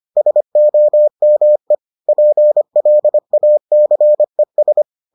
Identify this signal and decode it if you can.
Morse code, heard as a keyed tone, about 25 wpm, 600 Hz. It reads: SOME PLACES